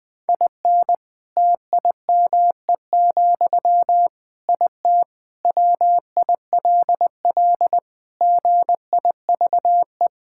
Morse code, heard as a keyed tone, about 20 wpm, 700 Hz. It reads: IN TIME, IT WILL GIVE
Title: Sentences from Top 1600 Words 20wpm